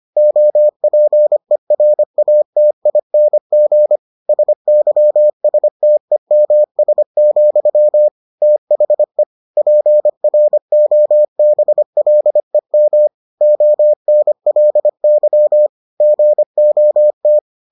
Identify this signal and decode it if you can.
Morse code, heard as a keyed tone, about 25 wpm, 600 Hz. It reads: OPERATING SYSTEMS, THE PROBLEM ONLY GOT